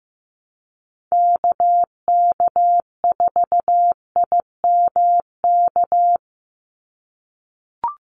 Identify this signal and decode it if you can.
Morse code, heard as a keyed tone, about 15 wpm, 700 Hz. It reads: KK4IMK
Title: USA Call Signs 15wpm